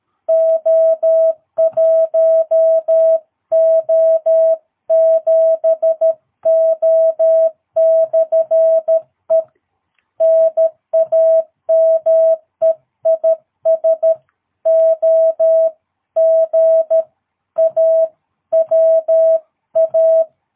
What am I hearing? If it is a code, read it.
Morse code, heard as a keyed tone, about 13 wpm, 660 Hz. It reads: O1O7O/E NAME IS OGAWA
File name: jh8cbh / 2022cw